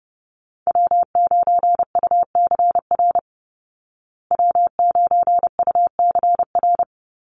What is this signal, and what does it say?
Morse code, heard as a keyed tone, about 28 wpm, 700 Hz. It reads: W9UCR W9UCR
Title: Call Signs Repeated 3x - 30wpm